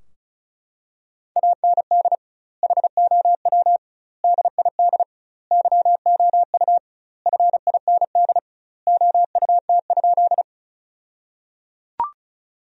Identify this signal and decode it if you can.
Morse code, heard as a keyed tone, about 36 wpm, 700 Hz. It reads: AND HOW DID YOU FIND OUT?
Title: 2x Word-Spacing - Sentences from Top 200 Words - 35wpm